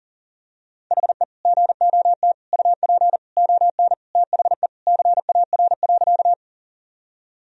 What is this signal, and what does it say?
Morse code, heard as a keyed tone, about 40 wpm, 700 Hz. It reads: HE GOT UP ON THE CAR.
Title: Sentences from Top 1000 Words - Review Part 1 - 40wpm